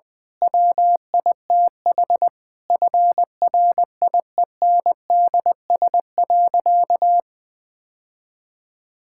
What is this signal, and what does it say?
Morse code, heard as a keyed tone, about 19 wpm, 700 Hz. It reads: WITH FRIENDS.
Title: Difficult Sentences with Hints - 20wpm